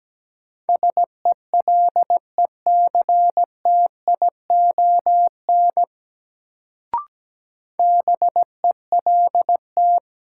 Morse code, heard as a keyed tone, about 17 wpm, 700 Hz. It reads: SELECTION BELT
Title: Sentences from Top 1400 Words - New Word Drill Alt - 17wpm